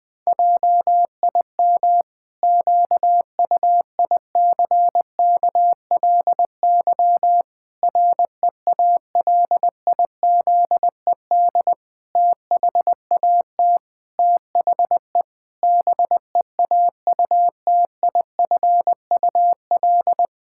Morse code, heard as a keyed tone, about 21 wpm, 700 Hz. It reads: JIM QUICKLY REALIZED THAT THE BEAUTIFUL
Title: Pangram Challenge v2 - 20wpm